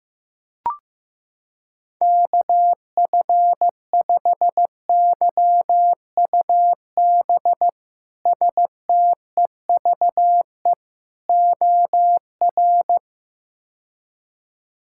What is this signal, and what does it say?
Morse code, heard as a keyed tone, about 16 wpm, 700 Hz. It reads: KF5YUB STEVE OR
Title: K1USN SST QSO Practice - 15wpm